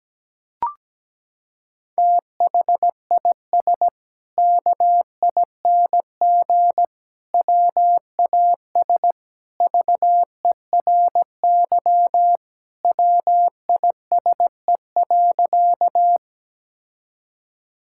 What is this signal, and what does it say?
Morse code, heard as a keyed tone, about 17 wpm, 700 Hz. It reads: THIS KING WAS VERY WISE.
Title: Sentences from Top 1600 Words 17wpm